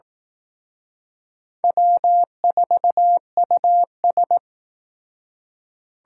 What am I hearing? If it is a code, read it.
Morse code, heard as a keyed tone, about 18 wpm, 700 Hz. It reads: W4US
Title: Speed Racing - USA 1x2 Call Signs 15wpm